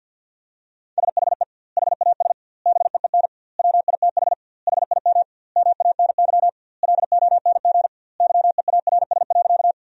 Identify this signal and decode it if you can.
Morse code, heard as a keyed tone, about 50 wpm, 700 Hz. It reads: SHE HAS BEEN WITH HIM MANY LONG YEARS.
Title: 2x Word-Spacing - Sentences from Top 200 Words - 50wpm